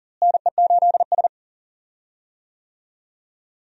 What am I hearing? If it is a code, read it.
Morse code, heard as a keyed tone, about 40 wpm, 700 Hz. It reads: NE8S
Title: USA Call Signs - 4 Characters - Repeated 3x - 40wpm